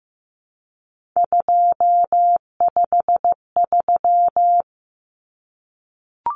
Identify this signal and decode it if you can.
Morse code, heard as a keyed tone, about 15 wpm, 700 Hz. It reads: 253
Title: Numbers - 3-Digit 15wpm